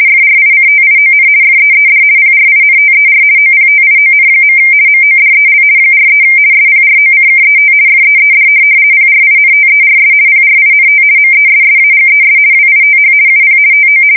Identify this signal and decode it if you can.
RTTY, about 50 baud. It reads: FRICTION FEED PRINTER OPTION. MODEL 43 SPROCKET FEED PRINTERS PRINT MESSAGES WITH UP TO
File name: fsk170-50B / teletype-corp